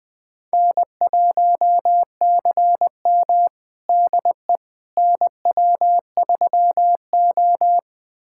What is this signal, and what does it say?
Morse code, heard as a keyed tone, about 20 wpm, 700 Hz. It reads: N1CM DE NW3O
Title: USA Callsign DE Callsign - 4 Characters - 20wpm